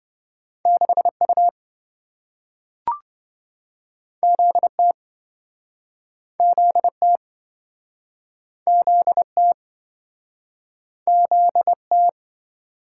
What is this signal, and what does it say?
Morse code, heard as a keyed tone, about 27 wpm, 700 Hz. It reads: 6U ZT ZT ZT ZT